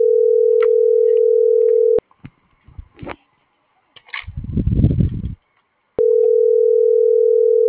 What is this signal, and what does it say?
Ringback tone (North America), 440 + 480 Hz.